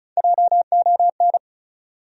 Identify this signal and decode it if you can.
Morse code, heard as a keyed tone, about 35 wpm, 700 Hz. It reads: JON
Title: USA Common Names - Max 4 Letters - 35wpm